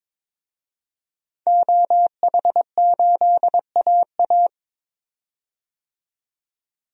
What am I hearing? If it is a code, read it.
Morse code, heard as a keyed tone, about 22 wpm, 700 Hz. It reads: OH8AA
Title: DXCC-100 Call Signs - 5 Characters - 22wpm